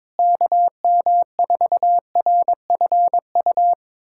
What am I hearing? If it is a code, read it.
Morse code, heard as a keyed tone, about 23 wpm, 700 Hz. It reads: KM4RFU
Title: USA Callsign DE Callsign 22wpm